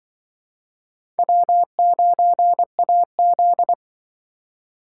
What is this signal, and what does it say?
Morse code, heard as a keyed tone, about 24 wpm, 700 Hz. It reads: W9AZ
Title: Speed Racing - USA 1x2 Call Signs 20wpm